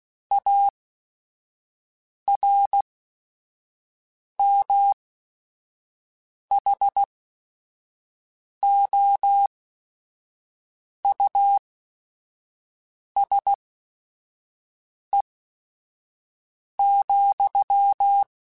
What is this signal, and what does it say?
Morse code, heard as a keyed tone, about 16 wpm, spaced as for 5 wpm, 800 Hz. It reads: ARMHOUSE,